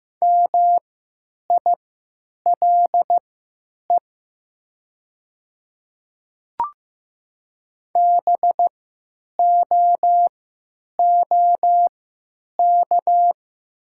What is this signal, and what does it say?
Morse code, heard as a keyed tone, about 15 wpm, spaced as for 9 wpm, 700 Hz. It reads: MILE BOOK
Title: 3x Character Spacing - Top 500 Words - 15wpm